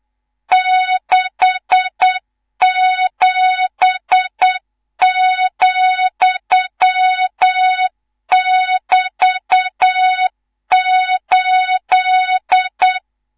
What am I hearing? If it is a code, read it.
Morse code, heard as a keyed tone, about 8 wpm, 3700 Hz, with noise. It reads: 67,<BT>8
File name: Morze-HA2MN / hang / 19 GYAKORLAT